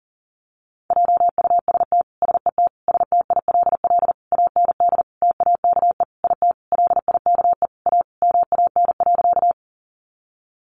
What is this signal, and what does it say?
Morse code, heard as a keyed tone, about 37 wpm, 700 Hz. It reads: JUST SET STILL AND TAKE IT LIKE A MAN.
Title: Sentences from Top 300 Words 40wpm